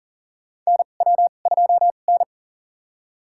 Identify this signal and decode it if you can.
Morse code, heard as a keyed tone, about 40 wpm, spaced as for 29 wpm, 700 Hz. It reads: NW2N
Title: USA Call Signs - 4 Characters - 2x Character Spacing - 40wpm